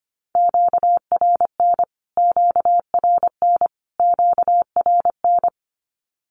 Morse code, heard as a keyed tone, about 25 wpm, 700 Hz. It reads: QRN QRN QRN
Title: Q-Codes Repeated 3x 25wpm